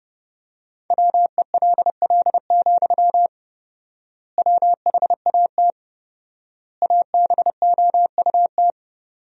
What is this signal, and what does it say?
Morse code, heard as a keyed tone, about 30 wpm, 700 Hz. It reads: WELL, WHAT ABOUT
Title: 4x Word-Spacing - Sentences from Top 500 Words - 30wpm